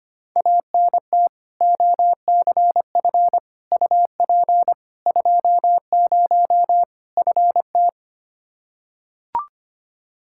Morse code, heard as a keyed tone, about 25 wpm, 700 Hz. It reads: ANT OCF UP 20 FT
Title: Practice QSO - Exchange 2 - Info - 25wpm